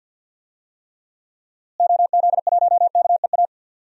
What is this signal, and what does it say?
Morse code, heard as a keyed tone, about 52 wpm, 700 Hz. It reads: OZ1KEA
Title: DXCC-100 Call Signs - 6 Characters - Repeated 2x - 50wpm